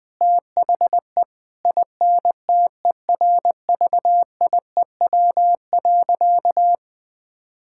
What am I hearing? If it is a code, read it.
Morse code, heard as a keyed tone, about 20 wpm, 700 Hz. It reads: THE INTERVIEW.
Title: Sentences from Top 1600 Words 20wpm